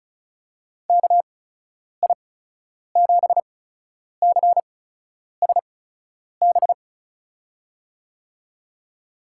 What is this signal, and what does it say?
Morse code, heard as a keyed tone, about 34 wpm, spaced as for 10 wpm, 700 Hz. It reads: KI7CSB
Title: USA Call Signs - 6 Characters - 8x Character Spacing - 35wpm